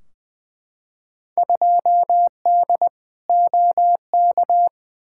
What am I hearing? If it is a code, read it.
Morse code, heard as a keyed tone, about 20 wpm, 700 Hz. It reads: 2D OK
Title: Field Day Exchanges Encore - 20wpm